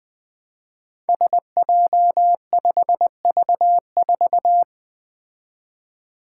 Morse code, heard as a keyed tone, about 20 wpm, 700 Hz. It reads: SJ5V4
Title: Five Challenging Letter-Number Combinations 20wpm